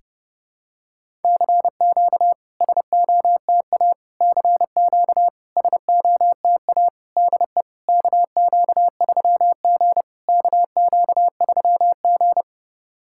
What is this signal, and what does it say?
Morse code, heard as a keyed tone, about 30 wpm, 700 Hz. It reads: CQ SOTA CQ SOTA DE KQ3G KQ3G
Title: SOTA QSO Practice 30wpm